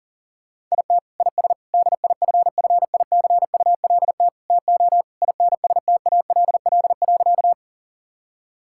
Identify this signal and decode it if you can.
Morse code, heard as a keyed tone, about 40 wpm, 700 Hz. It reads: IT IS DIFFICULT TO INSTALL.
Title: Sentences from Top 1600 Words 40wpm